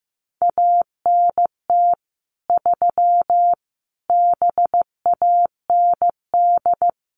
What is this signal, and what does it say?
Morse code, heard as a keyed tone, about 15 wpm, 700 Hz. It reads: ANT 3 BAND
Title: Common QSO Phrases 15wpm